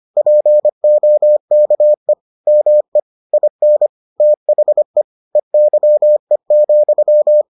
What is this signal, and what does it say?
Morse code, heard as a keyed tone, about 25 wpm, 600 Hz. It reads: POKE ME IN THE EYE,